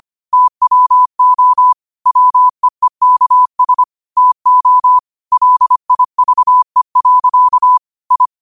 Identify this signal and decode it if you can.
Morse code, heard as a keyed tone, about 25 wpm, 1000 Hz. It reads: TWO WEEKS TO LIVE. I
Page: New quotes every day in morse code at 25 Words per minute.